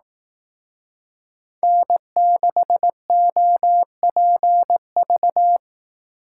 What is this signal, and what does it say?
Morse code, heard as a keyed tone, about 19 wpm, 700 Hz. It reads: N6OPV
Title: Speed Racing - USA 1x3 Call Signs 15wpm